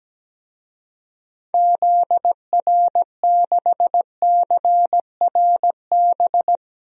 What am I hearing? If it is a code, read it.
Morse code, heard as a keyed tone, about 17 wpm, 700 Hz. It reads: ZR6CRB
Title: DXCC-100 Call Signs - 6 Characters - 17wpm